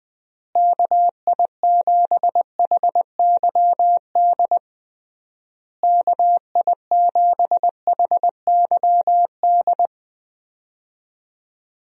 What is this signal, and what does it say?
Morse code, heard as a keyed tone, about 21 wpm, 700 Hz. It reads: KI7HYD KI7HYD
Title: USA Call Signs - 6 Characters - Repeated 3x - 20wpm